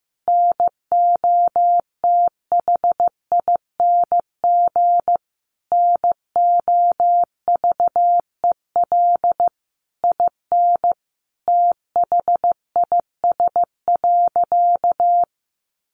Morse code, heard as a keyed tone, about 15 wpm, 700 Hz. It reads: NOTHING NOVEL IN THIS.
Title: Sentences from Top 1300 Words 15wpm